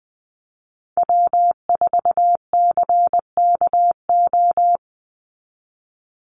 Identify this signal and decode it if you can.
Morse code, heard as a keyed tone, about 20 wpm, 700 Hz. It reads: W4CKO
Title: USA Call Signs 20wpm